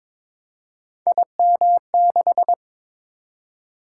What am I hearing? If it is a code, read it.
Morse code, heard as a keyed tone, about 23 wpm, 700 Hz. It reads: IM6
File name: practice / 22wpm / Three Letter-Number Combinations - 22wpm